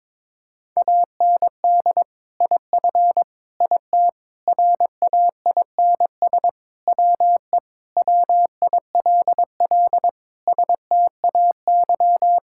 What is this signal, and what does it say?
Morse code, heard as a keyed tone, about 22 wpm, 700 Hz. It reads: AND IF IT RAINS WE WILL STAY
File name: practice / 22wpm / Sentences with 2-Word Phrases - 22wpm